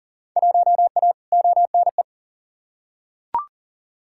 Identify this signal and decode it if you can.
Morse code, heard as a keyed tone, about 39 wpm, 700 Hz. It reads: 1A ONE
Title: Field Day Exchanges Encore - 40wpm